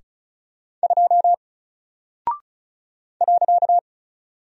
Morse code, heard as a keyed tone, about 35 wpm, 700 Hz. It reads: 2 .
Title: Single Character (Letter:Number:Punctuation:Prosign) 35wpm